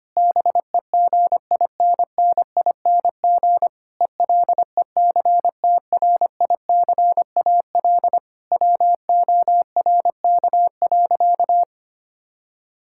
Morse code, heard as a keyed tone, about 25 wpm, 700 Hz. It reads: BEGINNING ELECTRICAL WORK.
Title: Sentences from Top 1000 Words - Review Part 6 - 25wpm